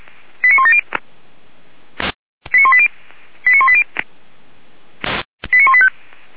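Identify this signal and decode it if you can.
Five-tone selective call (ZVEI), heard as quick runs of five tones, tones of about 70 ms. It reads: 89170 89180 89180 89185
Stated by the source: ZVEI five-tone call received by SDR, from audio.f85360.feuerwehr-orlamuende.org fire brigade server.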